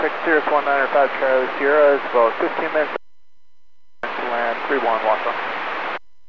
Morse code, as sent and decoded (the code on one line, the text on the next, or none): none